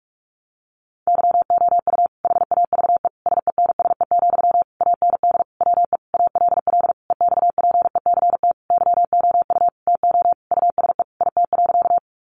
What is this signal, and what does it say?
Morse code, heard as a keyed tone, about 44 wpm, 700 Hz. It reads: YOU HAVE SENSE, AND WE ALL EXPECT YOU TO USE IT.
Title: Sentences from Top 1100 Words 45wpm